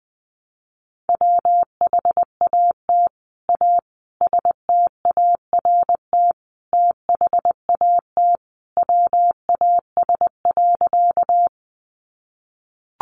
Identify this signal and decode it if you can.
Morse code, heard as a keyed tone, about 20 wpm, 700 Hz. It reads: WHAT A START THAT WAS.
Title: Sentences from Top 300 Words 20wpm